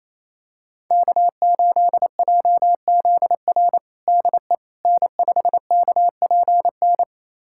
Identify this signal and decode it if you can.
Morse code, heard as a keyed tone, about 28 wpm, 700 Hz. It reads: K8JZR DE N5KPN
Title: USA Callsign DE Callsign - 5 Characters - 28wpm